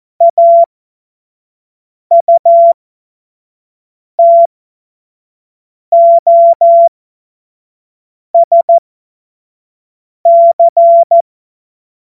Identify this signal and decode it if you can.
Morse code, heard as a keyed tone, about 14 wpm, spaced as for 5 wpm, 680 Hz. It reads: AUTOSC
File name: Morsekurs / Witze / Witz 26. April 2022 25 F70